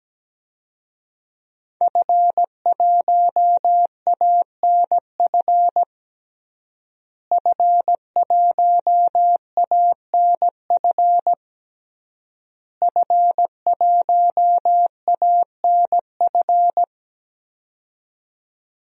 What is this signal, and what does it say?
Morse code, heard as a keyed tone, about 17 wpm, 700 Hz. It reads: F1ANF F1ANF F1ANF
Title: DXCC-100 Call Signs - 5 Characters - Repeated 3x - 17wpm